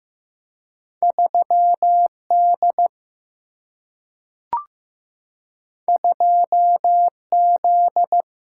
Morse code, heard as a keyed tone, about 15 wpm, 700 Hz. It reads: 3D 2Z
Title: Single-Letter Single-Number 15wpm